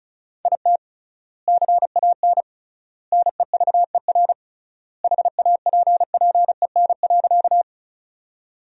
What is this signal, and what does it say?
Morse code, heard as a keyed tone, about 34 wpm, 700 Hz. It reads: IT CAN NEVER HAPPEN.
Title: 3x Word-Spacing - Sentences from Top 400 Words - 35wpm